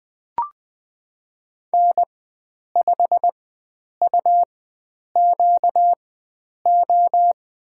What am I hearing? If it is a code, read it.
Morse code, heard as a keyed tone, about 20 wpm, spaced as for 9 wpm, 700 Hz. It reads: N5UQO
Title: USA Call Signs - 5 Characters - 4x Character Spacing - 20wpm